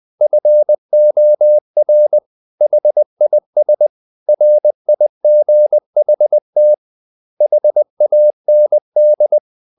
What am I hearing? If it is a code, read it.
Morse code, heard as a keyed tone, about 20 wpm, 600 Hz. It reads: FOR HIS RIGHT HAND